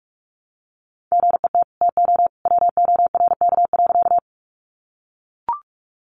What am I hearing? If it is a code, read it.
Morse code, heard as a keyed tone, about 44 wpm, 700 Hz. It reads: GET TO WORK.
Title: Sentences from Top 200 Words 45wpm